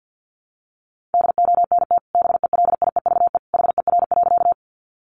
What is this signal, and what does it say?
Morse code, heard as a keyed tone, about 50 wpm, 700 Hz. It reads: DONT BELIEVE HER.
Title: Sentences from Top 1000 Words 50wpm